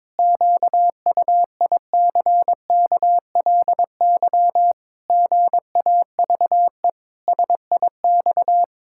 Morse code, heard as a keyed tone, about 22 wpm, 700 Hz. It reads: QUICKLY GAVE SIX
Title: Pangram Challenge v2 - 22wpm